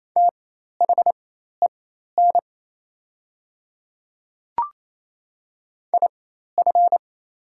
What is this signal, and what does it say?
Morse code, heard as a keyed tone, about 28 wpm, spaced as for 13 wpm, 700 Hz. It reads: THEN IF
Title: 4x Character Spacing - Top 100 Words - 28wpm